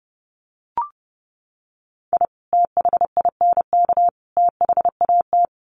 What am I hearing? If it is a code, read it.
Morse code, heard as a keyed tone, about 30 wpm, 700 Hz. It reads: I THINK THAT
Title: Sets of 3 Words 30wpm